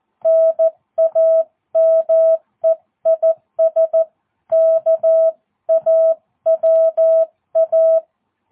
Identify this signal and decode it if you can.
Morse code, heard as a keyed tone, about 14 wpm, 650 Hz. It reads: NAME IS KAWA
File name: jh8cbh / 2022cw